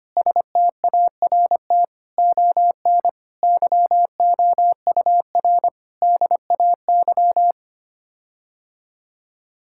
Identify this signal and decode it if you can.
Morse code, heard as a keyed tone, about 25 wpm, 700 Hz. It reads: START ON YOUR DAY
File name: Sets of 9 Words - 25wpm